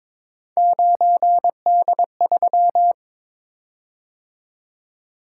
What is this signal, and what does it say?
Morse code, heard as a keyed tone, about 22 wpm, 700 Hz. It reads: 9D3
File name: practice / 22wpm / Three Letter-Number Combinations - 22wpm